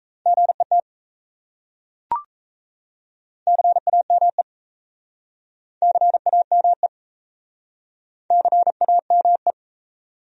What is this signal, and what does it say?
Morse code, heard as a keyed tone, about 38 wpm, 700 Hz. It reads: GET CAME CAME CAME